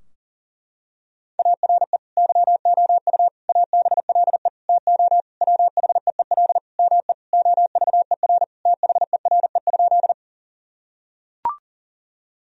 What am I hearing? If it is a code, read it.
Morse code, heard as a keyed tone, about 37 wpm, 700 Hz. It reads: ARE YOU ABLE TO WHEEL ME OVER THERE?
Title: Sentences from Top 1000 Words - Review Part 3 - 40wpm